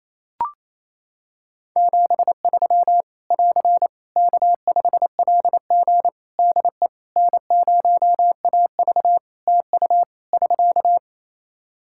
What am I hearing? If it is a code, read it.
Morse code, heard as a keyed tone, about 28 wpm, 700 Hz. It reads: 73 <AR> K5LG DE N0AV TU <SK>
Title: K1USN SST QSO Practice 28wpm